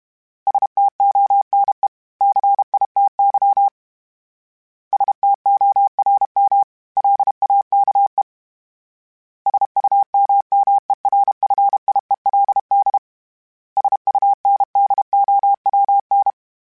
Morse code, heard as a keyed tone, about 32 wpm, 800 Hz. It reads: STONE CITY STORM LAKE SUMMERFIELD SUNDOWN
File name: fromSMs / FromSMs mp3s / Cities@30